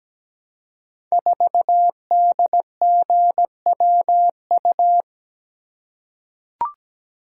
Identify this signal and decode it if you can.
Morse code, heard as a keyed tone, about 17 wpm, 700 Hz. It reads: 4DGWU
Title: Code Groups - 5 Letter:Numbers - 17wpm